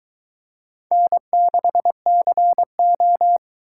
Morse code, heard as a keyed tone, about 23 wpm, 700 Hz. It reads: N6CO